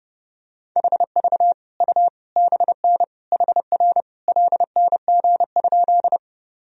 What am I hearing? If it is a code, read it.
Morse code, heard as a keyed tone, about 32 wpm, 700 Hz. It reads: HV U BN HR LNG?
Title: Sentences from Top 200 Words and Abbreviations 30wpm